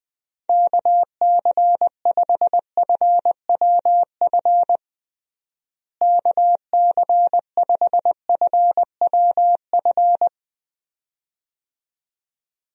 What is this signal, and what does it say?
Morse code, heard as a keyed tone, about 20 wpm, 700 Hz. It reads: KC5FWF KC5FWF
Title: USA Call Signs - 6 Characters - Repeated 2x - 20wpm